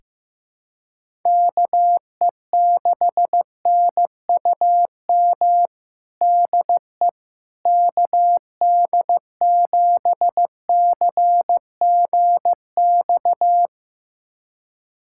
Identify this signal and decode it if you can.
Morse code, heard as a keyed tone, about 15 wpm, 700 Hz. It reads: KE6NUM DE KD7CGX
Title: USA Callsign DE Callsign - 6 Characters - 15wpm